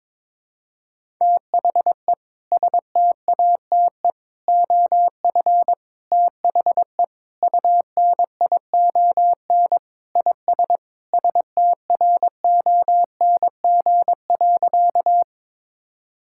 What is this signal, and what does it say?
Morse code, heard as a keyed tone, about 22 wpm, 700 Hz. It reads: THE STATE OF THE UNION IS STRONG.
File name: practice / 22wpm / Sentences with 3-Word Phrases - 22wpm